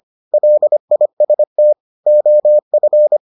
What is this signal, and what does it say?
Morse code, heard as a keyed tone, about 25 wpm, 600 Hz. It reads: LIST OF